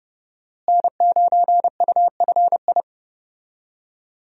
Morse code, heard as a keyed tone, about 30 wpm, 700 Hz. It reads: N9UFI
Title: USA Call Signs - 5 Characters - Repeated 3x - 30wpm